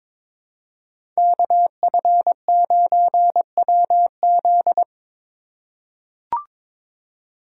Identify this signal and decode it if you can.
Morse code, heard as a keyed tone, about 22 wpm, 700 Hz. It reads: KF9WZ
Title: USA Call Signs - 5 Characters - 4x Character Spacing - 22wpm